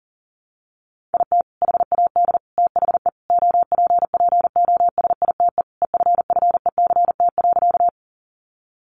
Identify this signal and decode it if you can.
Morse code, heard as a keyed tone, about 40 wpm, 700 Hz. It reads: IT HAD THE OPPOSITE EFFECT.
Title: Sentences from Top 1300 Words 40wpm